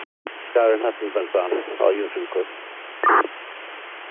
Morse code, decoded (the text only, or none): none